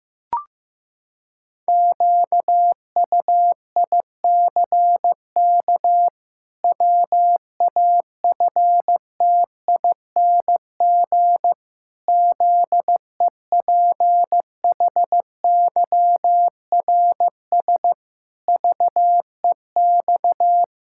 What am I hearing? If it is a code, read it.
Morse code, heard as a keyed tone, about 16 wpm, 700 Hz. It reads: QUICK WAFTING ZEPHYRS VEX
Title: Pangram Challenge v2 - 15wpm